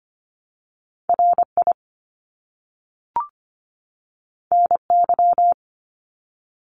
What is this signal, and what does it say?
Morse code, heard as a keyed tone, about 25 wpm, 700 Hz. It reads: RI NY
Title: US States 25wpm